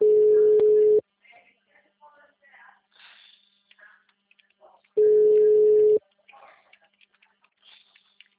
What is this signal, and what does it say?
Ringback tone (Europe, 425 Hz), with speech.